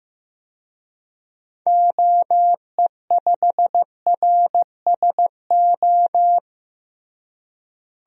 Morse code, heard as a keyed tone, about 15 wpm, 700 Hz. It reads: OE5RSO
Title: DXCC-100 Call Signs - 6 Characters - Repeated 2x - 15wpm